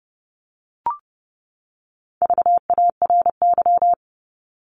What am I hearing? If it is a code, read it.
Morse code, heard as a keyed tone, about 30 wpm, 700 Hz. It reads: VARY
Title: Top 1001-1250 Words 30wpm